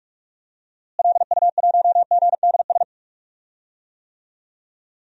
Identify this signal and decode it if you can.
Morse code, heard as a keyed tone, about 46 wpm, 700 Hz. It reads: LU1GDS
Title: DXCC-100 Call Signs - 6 Characters - Repeated 3x - 45wpm